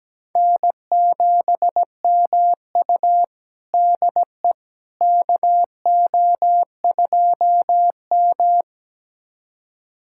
Morse code, heard as a keyed tone, about 17 wpm, 700 Hz. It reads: N7MU DE KO2M
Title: USA Callsign DE Callsign - 4 Characters - 17wpm